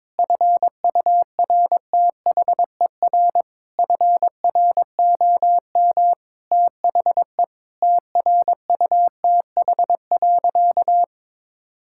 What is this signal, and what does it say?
Morse code, heard as a keyed tone, about 21 wpm, 700 Hz. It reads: FURTHER FROM THE TRUTH.
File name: practice / 22wpm / Sentences with 8-Word Phrases - 22wpm